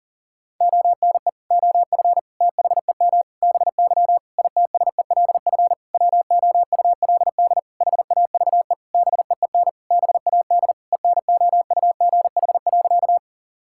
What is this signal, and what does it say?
Morse code, heard as a keyed tone, about 40 wpm, 700 Hz. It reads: ONE OF THEM BY ITSELF WOULD HAVE BEEN BAD ENOUGH.
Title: Sentences from Top 1000 Words - Review Part 5 - 40wpm